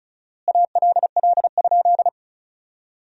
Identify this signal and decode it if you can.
Morse code, heard as a keyed tone, about 36 wpm, 700 Hz. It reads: ALL?
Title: 4x Word-Spacing - Sentences from Top 200 Words - 35wpm